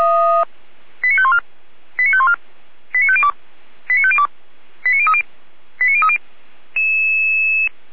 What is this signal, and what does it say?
Five-tone selective call (ZVEI), heard as quick runs of five tones, tones of about 70 ms. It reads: 89514 89514 89502 89502 89030 89030